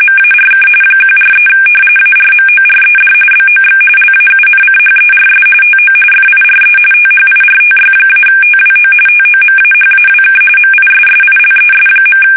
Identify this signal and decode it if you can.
RTTY, about 45.45 baud. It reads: MARK 143 ARMORED BOX LAUNCHERS TO FIRE THIRTY-TWO TOMAHAWK MISSILES.
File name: fsk850 / battleshipnj